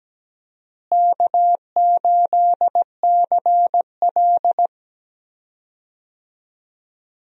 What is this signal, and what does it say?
Morse code, heard as a keyed tone, about 17 wpm, 700 Hz. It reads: K8CL
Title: USA Call Signs - 4 Characters - Repeated 2x - 17wpm